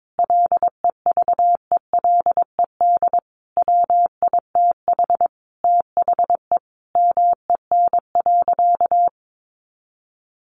Morse code, heard as a keyed tone, about 22 wpm, 700 Hz. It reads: LEVELED WITH THE MEN.
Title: A Sound Of Thunder 22wpm